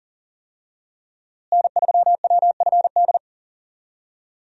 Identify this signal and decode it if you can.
Morse code, heard as a keyed tone, about 40 wpm, 700 Hz. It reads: N3WFD
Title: USA Call Signs - 5 Characters - Repeated 2x - 40wpm